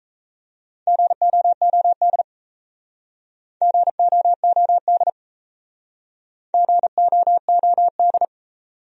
Morse code, heard as a keyed tone, about 37 wpm, 700 Hz. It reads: GOOD GOOD GOOD